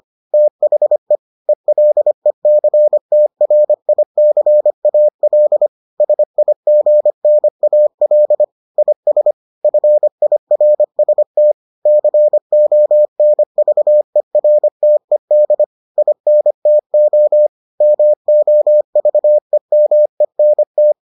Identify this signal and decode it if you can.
Morse code, heard as a keyed tone, about 25 wpm, 600 Hz. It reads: THE ELECTRICAL SIGNAL IS FIRST CONVERTED INTO MOVEMENT